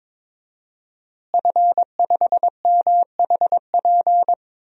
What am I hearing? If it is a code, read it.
Morse code, heard as a keyed tone, about 21 wpm, 700 Hz. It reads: F5MHP
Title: DXCC-100 Call Signs - 5 Characters - Repeated 2x - 22wpm